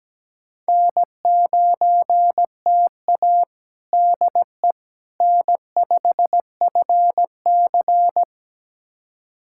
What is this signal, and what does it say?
Morse code, heard as a keyed tone, about 17 wpm, 700 Hz. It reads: N9TA DE N5FC
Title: USA Callsign DE Callsign - 4 Characters - 17wpm